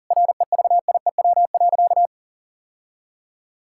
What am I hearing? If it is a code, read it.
Morse code, heard as a keyed tone, about 37 wpm, 700 Hz. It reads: REVIEW.
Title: Sentences from Top 1000 Words - Review Part 6 - 40wpm